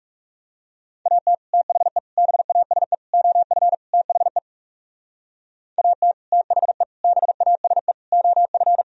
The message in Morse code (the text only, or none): AT THE BASE OF THE AT THE BASE OF